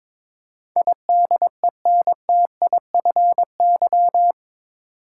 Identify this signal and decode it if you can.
Morse code, heard as a keyed tone, about 22 wpm, 700 Hz. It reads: IDENTIFY
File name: practice / 22wpm / Eight Letter Words - 22wpm